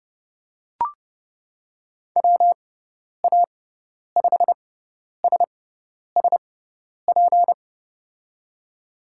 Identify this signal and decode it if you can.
Morse code, heard as a keyed tone, about 31 wpm, spaced as for 10 wpm, 700 Hz. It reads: WA5SSP
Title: USA Call Signs - 6 Characters - 6x Character Spacing - 30wpm